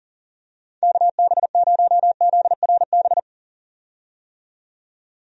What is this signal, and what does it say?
Morse code, heard as a keyed tone, about 40 wpm, 700 Hz. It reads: KB0ZRB
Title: USA Call Signs - 6 Characters - 40wpm